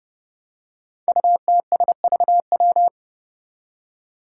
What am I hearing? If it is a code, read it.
Morse code, heard as a keyed tone, about 32 wpm, 700 Hz. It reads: UTSVW
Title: Five Challenging Letter-Number Combinations 30wpm